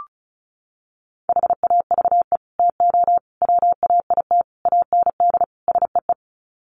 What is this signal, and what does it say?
Morse code, heard as a keyed tone, about 34 wpm, 700 Hz. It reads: HAVE TO WAIT AND SEE
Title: Sets of 5 Words Encore2 35wpm